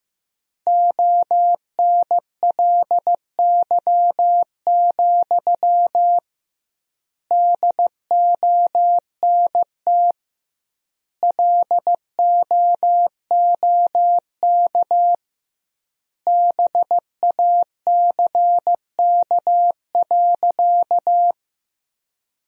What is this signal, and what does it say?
Morse code, heard as a keyed tone, about 15 wpm, 700 Hz. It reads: ONLY, DONT LOOK BACK.
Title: 2x Word-Spacing - Sentences from Top 300 Words - 15wpm